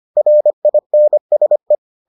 Morse code, heard as a keyed tone, about 25 wpm, 600 Hz. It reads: RINSE